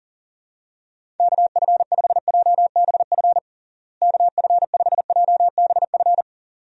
Morse code, heard as a keyed tone, about 40 wpm, 700 Hz. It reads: KF5JBF KF5JBF
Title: USA Call Signs - 6 Characters - Repeated 2x - 40wpm